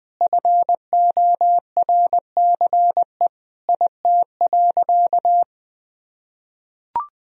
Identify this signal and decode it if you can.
Morse code, heard as a keyed tone, about 20 wpm, 700 Hz. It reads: FORCE IT.
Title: Sentences from Top 1000 Words - Review Part 3 - 20wpm